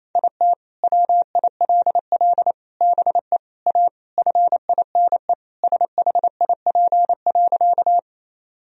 Morse code, heard as a keyed tone, about 28 wpm, 700 Hz. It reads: IT WILL BE A FINE SHIP.
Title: Sentences from Top 1000 Words - Review Part 2 - 28wpm